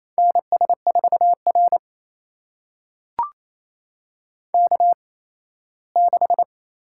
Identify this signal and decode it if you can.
Morse code, heard as a keyed tone, about 27 wpm, spaced as for 17 wpm, 700 Hz. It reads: NS4R K6
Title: USA Call Signs - 4 Characters - 8x Character Spacing - 28wpm